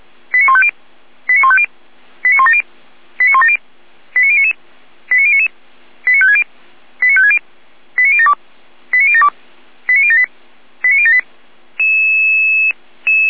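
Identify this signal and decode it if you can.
Five-tone selective call (ZVEI), heard as quick runs of five tones, tones of about 70 ms. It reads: 89150 89150 89170 89170 89099 89099 89560 89560 89072 89072 89078 89078
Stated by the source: Five-tone ZVEI selective call received by SDR.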